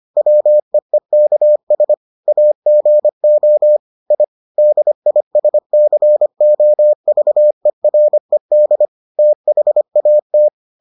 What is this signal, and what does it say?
Morse code, heard as a keyed tone, about 25 wpm, 600 Hz. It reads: WEEKS AGO I DISCOVERED THAT